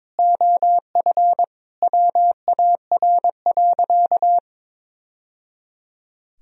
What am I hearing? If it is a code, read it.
Morse code, heard as a keyed tone, about 22 wpm, 700 Hz. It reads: OF WAR.
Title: Sentences from Top 1000 Words - Review Part 3 - 22wpm